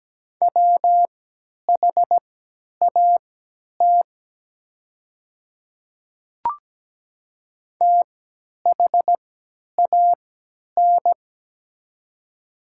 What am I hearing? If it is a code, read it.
Morse code, heard as a keyed tone, about 17 wpm, spaced as for 10 wpm, 700 Hz. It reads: WHAT THAN
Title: 3x Character Spacing - Top 100 Words - 17wpm